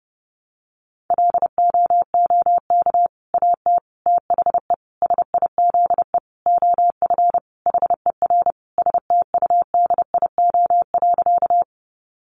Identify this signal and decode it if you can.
Morse code, heard as a keyed tone, about 30 wpm, 700 Hz. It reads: LOOK AT THE SIZE OF HER STUDIO.
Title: Sentences from Top 1200 Words 30wpm